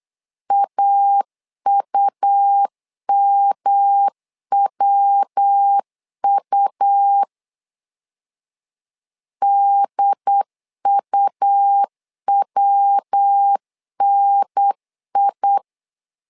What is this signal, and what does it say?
Morse code, heard as a keyed tone, about 8 wpm, 800 Hz. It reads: AUMWU DUWNI
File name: LEZIONI / 3 MWRO / 3 MW 8wpm 2501